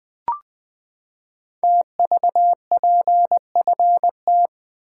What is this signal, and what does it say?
Morse code, heard as a keyed tone, about 20 wpm, 700 Hz. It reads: TVPFT
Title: Code Groups - 5 Letters - 20wpm